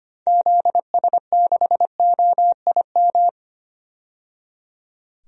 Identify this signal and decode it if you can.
Morse code, heard as a keyed tone, about 25 wpm, 700 Hz. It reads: ZS6OIM
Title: DXCC-100 Call Signs - 6 Characters - 25wpm